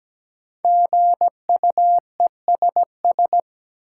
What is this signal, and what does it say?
Morse code, heard as a keyed tone, about 17 wpm, 700 Hz. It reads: GUESS
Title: Sentences from Top 1500 Words - New Word Drill - 17wpm